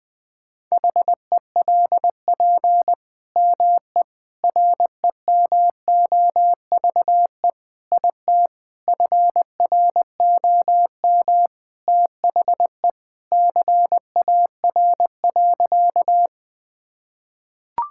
Morse code, heard as a keyed tone, about 20 wpm, 700 Hz. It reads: HELP ME REMOVE IT FROM THE CAR.
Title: Sentences from Top 1000 Words - Review Part 5 - 20wpm